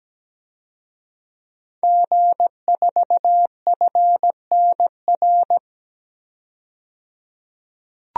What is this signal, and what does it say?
Morse code, heard as a keyed tone, about 17 wpm, 700 Hz. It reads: G4FNR
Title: DXCC-100 Call Signs - 5 Characters - 17wpm